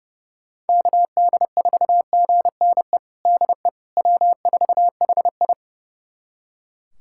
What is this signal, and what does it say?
Morse code, heard as a keyed tone, about 30 wpm, 700 Hz. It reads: KD4GNE DE W4HI
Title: USA Callsign DE Callsign 30wpm